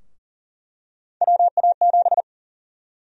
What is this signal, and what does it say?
Morse code, heard as a keyed tone, about 40 wpm, 700 Hz. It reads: WA7
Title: USA Call Sign Prefixes - 40wpm